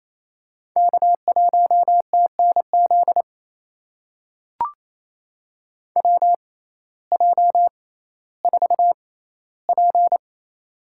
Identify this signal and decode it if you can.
Morse code, heard as a keyed tone, about 27 wpm, spaced as for 15 wpm, 700 Hz. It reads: K1TNZ WJ4P
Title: USA Call Signs - 6x Character Spacing - 28wpm